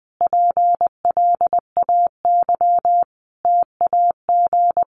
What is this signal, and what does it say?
Morse code, heard as a keyed tone, about 20 wpm, 700 Hz. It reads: PLAY TAG
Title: Additional Easy Phrases 20wpm